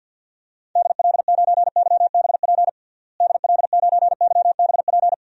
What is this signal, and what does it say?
Morse code, heard as a keyed tone, about 52 wpm, 700 Hz. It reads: DL9YBP DL9YBP
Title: DXCC-100 Call Signs - 6 Characters - Repeated 2x - 50wpm